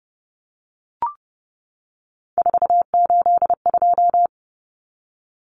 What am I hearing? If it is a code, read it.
Morse code, heard as a keyed tone, about 30 wpm, 700 Hz. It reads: 482
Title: Common Cut Numbers - 3-Digit 30wpm